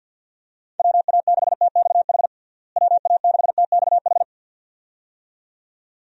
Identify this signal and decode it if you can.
Morse code, heard as a keyed tone, about 46 wpm, 700 Hz. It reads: WA6TXH WA6TXH
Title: USA Call Signs - 6 Characters - Repeated 2x - 50wpm